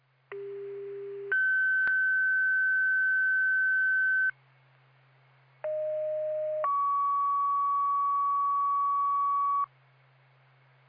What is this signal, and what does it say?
Two-tone page, heard as a one-second tone, then a three-second tone, about 400 then 1550 Hz, about 620 then 1130 Hz. It is heard from 0.3 to 9.7 s.